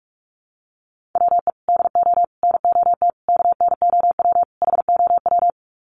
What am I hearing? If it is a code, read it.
Morse code, heard as a keyed tone, about 44 wpm, 700 Hz. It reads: WE DO NOT KNOW HOW
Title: Sets of 5 Words Encore 45wpm